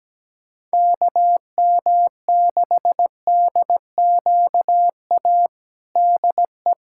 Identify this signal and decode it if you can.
Morse code, heard as a keyed tone, about 17 wpm, 700 Hz. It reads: KM6DQA DE
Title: USA Callsign DE Callsign 17wpm